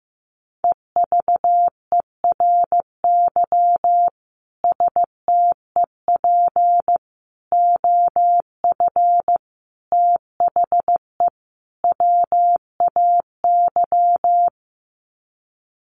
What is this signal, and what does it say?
Morse code, heard as a keyed tone, about 15 wpm, 700 Hz. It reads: EVERY STEP OF THE WAY
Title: Sets of 5 Words Encore2 15wpm